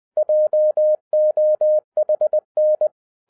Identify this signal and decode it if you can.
Morse code, heard as a keyed tone, about 20 wpm, 610 Hz. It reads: JOHN